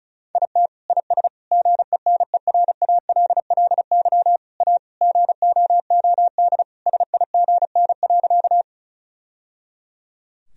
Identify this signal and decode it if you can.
Morse code, heard as a keyed tone, about 34 wpm, 700 Hz. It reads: IT IS GENERALLY A GOOD SIGN.
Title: Sentences from Top 1000 Words - Review Part 4 - 35wpm